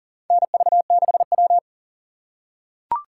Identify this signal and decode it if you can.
Morse code, heard as a keyed tone, about 40 wpm, 700 Hz. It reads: NV6W
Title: USA Call Signs - 4 Characters - 2x Character Spacing - 40wpm